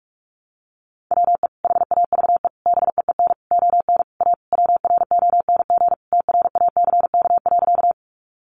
Morse code, heard as a keyed tone, about 44 wpm, 700 Hz. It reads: WE HAVE BEEN ON A WRONG TRACK.
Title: Sentences from Top 1100 Words 45wpm